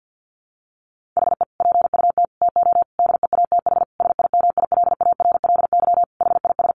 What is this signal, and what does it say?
Morse code, heard as a keyed tone, about 48 wpm, 700 Hz. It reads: HE PUT TO DEATH SIMILARLY HIS
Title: Sentences from Top 1500 Words 50wpm